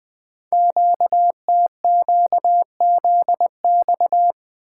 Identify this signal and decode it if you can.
Morse code, heard as a keyed tone, about 20 wpm, 700 Hz. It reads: QTQZX
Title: Code Groups - 5 Letters - 20wpm